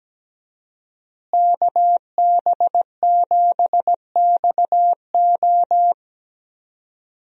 Morse code, heard as a keyed tone, about 17 wpm, 700 Hz. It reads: KB7XO
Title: USA Call Signs - 5 Characters - Repeated 3x - 17wpm